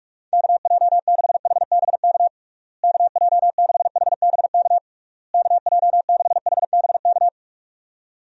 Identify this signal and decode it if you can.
Morse code, heard as a keyed tone, about 41 wpm, 700 Hz. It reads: KJ6HBK KJ6HBK KJ6HBK
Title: USA Call Signs - 6 Characters - Repeated 3x - 45wpm